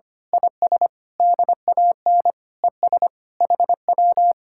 Morse code, heard as a keyed tone, about 24 wpm, 700 Hz. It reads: IS DAN ES HW
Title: Speed-Racing Common QSO Phrases - 25wpm